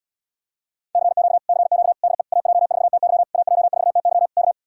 Morse code, heard as a keyed tone, about 77 wpm, 700 Hz. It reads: CQ CQ DE AJ4TY AJ4TY K